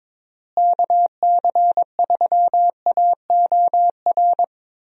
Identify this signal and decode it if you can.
Morse code, heard as a keyed tone, about 22 wpm, 700 Hz. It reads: KC3AOR
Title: USA Call Signs - 6 Characters - Repeated 2x - 22wpm